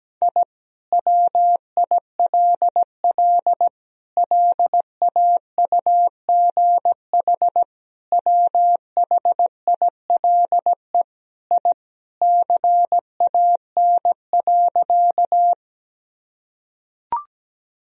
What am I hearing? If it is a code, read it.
Morse code, heard as a keyed tone, about 17 wpm, 700 Hz. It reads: I WILL LAUGH WHILE I CAN.
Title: Sentences from Top 1000 Words - Review Part 1 - 17wpm